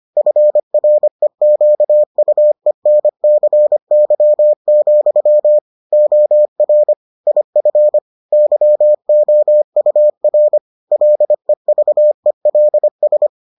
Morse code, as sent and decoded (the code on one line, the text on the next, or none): ..-. .-. --.- ..- . -. -.-. -.-- --..-- / --- .-. / .. ..-. / -.-- --- ..- .-. / .-.. ...- . .-.. ...
FREQUENCY, OR IF YOUR LEVELS